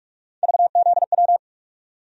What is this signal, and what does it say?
Morse code, heard as a keyed tone, about 43 wpm, 700 Hz. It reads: VZW
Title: USA Call Sign Postfixes - 3 Letters 45wpm